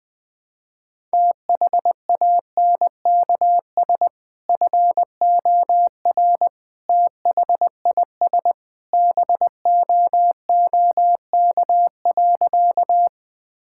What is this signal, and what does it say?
Morse code, heard as a keyed tone, about 20 wpm, 700 Hz. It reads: THANKS FOR THIS BOOK.
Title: Sentences with 2-Word Phrases - 20wpm